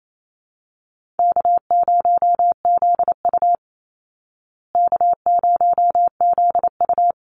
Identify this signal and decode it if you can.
Morse code, heard as a keyed tone, about 28 wpm, 700 Hz. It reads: K0ZU K0ZU
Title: Easy Call Signs Repeated 3x - 28wpm